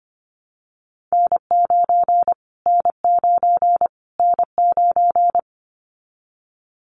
Morse code, heard as a keyed tone, about 25 wpm, 700 Hz. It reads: N9 N9 N9
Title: Callsign Prefixes for Top 25 DXCC - Repeated 3x - 25wpm